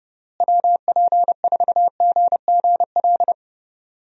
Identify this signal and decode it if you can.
Morse code, heard as a keyed tone, about 30 wpm, 700 Hz. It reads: WP4GGL
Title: USA Call Signs - 6 Characters - Repeated 3x - 30wpm